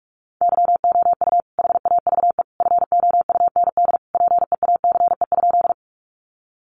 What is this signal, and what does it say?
Morse code, heard as a keyed tone, about 44 wpm, 700 Hz. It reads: YOU HAVE FOUND PEACE?
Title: Sentences from Top 1500 Words 45wpm